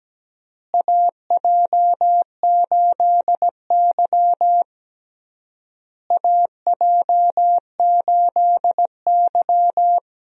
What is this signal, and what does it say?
Morse code, heard as a keyed tone, about 17 wpm, 700 Hz. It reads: AJ8Y AJ8Y
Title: USA Call Signs - 4 Characters - Repeated 3x - 17wpm